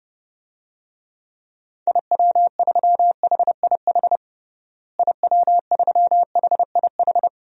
Morse code, heard as a keyed tone, about 30 wpm, 700 Hz. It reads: IW3HIH IW3HIH
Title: DXCC-100 Call Signs - 6 Characters - Repeated 2x - 30wpm